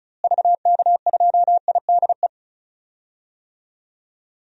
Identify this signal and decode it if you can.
Morse code, heard as a keyed tone, about 36 wpm, 700 Hz. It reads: VK2IDE
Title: DXCC-100 Call Signs - 6 Characters - Repeated 2x - 35wpm